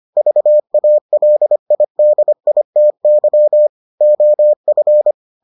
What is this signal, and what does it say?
Morse code, heard as a keyed tone, about 25 wpm, 600 Hz. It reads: VALIDITY OF